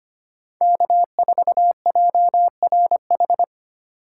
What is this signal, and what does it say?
Morse code, heard as a keyed tone, about 25 wpm, 700 Hz. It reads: K4JRH
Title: USA Call Signs - 5 Characters - 2x Character Spacing - 25wpm